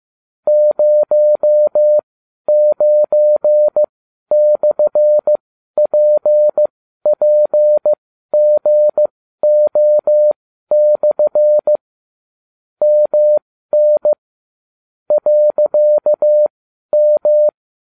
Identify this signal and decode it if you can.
Morse code, heard as a keyed tone, about 15 wpm, spaced as for 12 wpm, 600 Hz. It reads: 09/PPGO/ MN .M